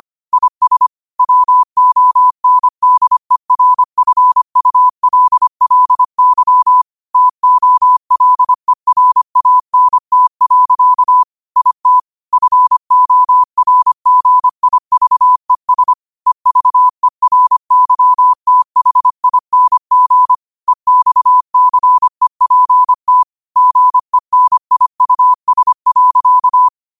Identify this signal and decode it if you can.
Morse code, heard as a keyed tone, about 25 wpm, 1000 Hz. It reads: IS WONDERFULLY TOLERANT. IT FORGIVES EVERYTHING EXCEPT GENIUS.
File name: qotdfiles / 25wpm